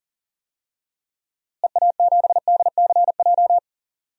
Morse code, heard as a keyed tone, about 40 wpm, 700 Hz. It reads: EA7DCJ
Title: DXCC-100 Call Signs - 6 Characters - Repeated 2x - 40wpm